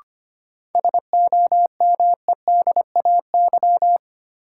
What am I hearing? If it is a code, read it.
Morse code, heard as a keyed tone, about 25 wpm, 700 Hz. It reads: SOMEDAY
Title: Sentences from Top 1600 Words - New Word Drill - 25wpm